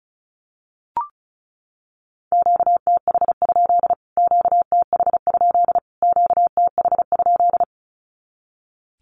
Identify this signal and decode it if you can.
Morse code, heard as a keyed tone, about 34 wpm, 700 Hz. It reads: QTH? QTH? QTH?
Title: Q-Codes Repeated 3x 35wpm